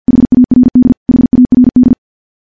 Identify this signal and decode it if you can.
Morse code, heard as a keyed tone, about 50 wpm, 250 Hz. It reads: HARD HARD
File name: morse / morse-training-50WPM